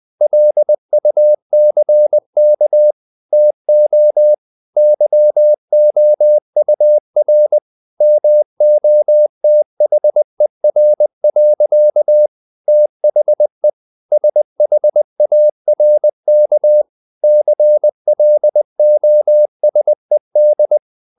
Morse code, heard as a keyed tone, about 20 wpm, 600 Hz. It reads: LUCK TO YOUR MOTHER. THE SHARK CLOSED
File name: the old man and the sea part 2 20wpm